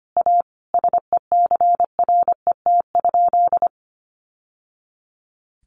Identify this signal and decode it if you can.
Morse code, heard as a keyed tone, about 25 wpm, 700 Hz. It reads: A SECRET?
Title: Sentences from Top 1200 Words 25wpm